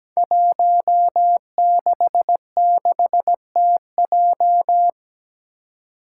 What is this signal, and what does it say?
Morse code, heard as a keyed tone, about 17 wpm, 700 Hz. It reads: 166TJ
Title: Five Challenging Letter-Number Combinations 17wpm